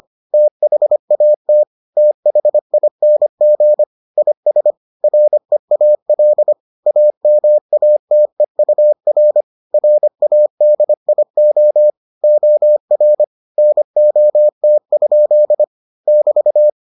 Morse code, heard as a keyed tone, about 25 wpm, 600 Hz. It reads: THAT THING IS REAL AMATEUR RADIO OR NOT? <BT>